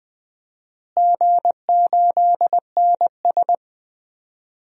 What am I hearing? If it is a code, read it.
Morse code, heard as a keyed tone, about 20 wpm, 700 Hz. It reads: G8NS
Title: DXCC-100 Call Signs - 4 Characters - Repeated 2x - 20wpm